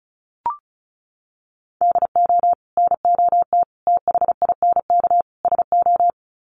Morse code, heard as a keyed tone, about 34 wpm, 700 Hz. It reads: DO NOT THINK SO
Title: Sets of 4 Words 35wpm